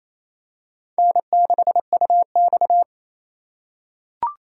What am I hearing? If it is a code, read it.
Morse code, heard as a keyed tone, about 28 wpm, 700 Hz. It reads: N6UX
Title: USA Call Signs - 4 Characters - 3x Character Spacing - 28wpm